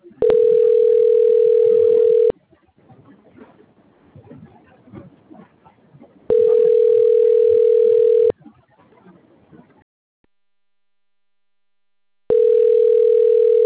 Ringback tone (North America), 440 + 480 Hz.
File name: rg-1600-unknown- - - .23980